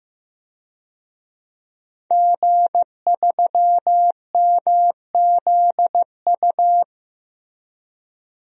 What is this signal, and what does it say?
Morse code, heard as a keyed tone, about 15 wpm, 700 Hz. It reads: G3MZU
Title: DXCC-100 Call Signs - 5 Characters - 15wpm